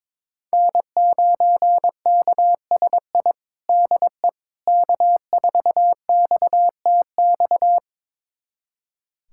Morse code, heard as a keyed tone, about 22 wpm, 700 Hz. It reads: N9KSI DE K4XTX
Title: USA Callsign DE Callsign - 5 Characters - 22wpm